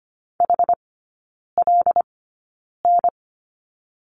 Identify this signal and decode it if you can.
Morse code, heard as a keyed tone, about 25 wpm, spaced as for 9 wpm, 700 Hz. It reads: HLN
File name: course / 25:10wpm / 093 Lesson 16e - Letter c - Three Characters 25wpm